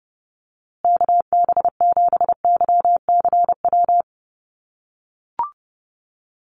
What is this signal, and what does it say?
Morse code, heard as a keyed tone, about 30 wpm, 700 Hz. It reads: KB7YCW
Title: USA Call Signs Repeated 2x 30wpm